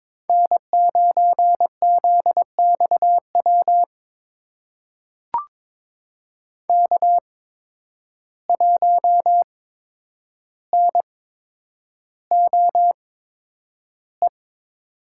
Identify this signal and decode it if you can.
Morse code, heard as a keyed tone, about 21 wpm, spaced as for 9 wpm, 700 Hz. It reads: N9ZXW K1NOE
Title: USA Call Signs - 5 Characters - 8x Character Spacing - 22wpm